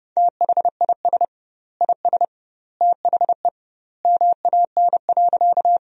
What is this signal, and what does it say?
Morse code, heard as a keyed tone, about 32 wpm, 700 Hz. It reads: THIS IS THE MAN.
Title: 2x Word-Spacing - Sentences from Top 200 Words - 30wpm